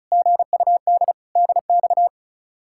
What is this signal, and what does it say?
Morse code, heard as a keyed tone, about 45 wpm, 700 Hz. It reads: GUD DX